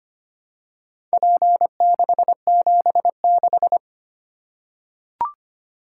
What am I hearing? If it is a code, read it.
Morse code, heard as a keyed tone, about 25 wpm, 700 Hz. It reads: P676
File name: practice / 25wpm / Four Letter-Number Combinations - 25wpm